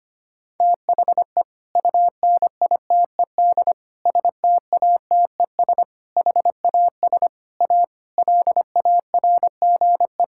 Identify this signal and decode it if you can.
Morse code, heard as a keyed tone, about 25 wpm, 700 Hz. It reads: THE UNITED STATES HAS A LARGE
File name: practice / 25wpm / Sentences with 4-Word Phrases - 25wpm